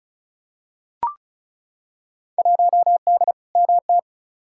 Morse code, heard as a keyed tone, about 34 wpm, 700 Hz. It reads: 1D MT
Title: Field Day Exchanges - 35wpm